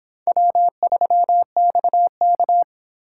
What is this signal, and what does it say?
Morse code, heard as a keyed tone, about 30 wpm, 700 Hz. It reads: W3XK